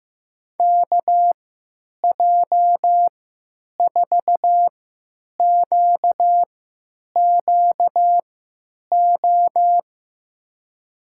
Morse code, heard as a keyed tone, about 15 wpm, spaced as for 9 wpm, 700 Hz. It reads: KJ4QQO
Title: USA Call Signs - 6 Characters - 3x Character Spacing - 15wpm